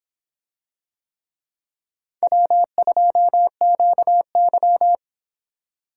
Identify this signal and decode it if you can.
Morse code, heard as a keyed tone, about 25 wpm, 700 Hz. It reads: W2QY